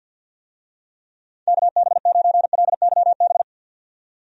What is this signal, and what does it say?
Morse code, heard as a keyed tone, about 51 wpm, 700 Hz. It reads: KB9LYB
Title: USA Call Signs - 6 Characters - 50wpm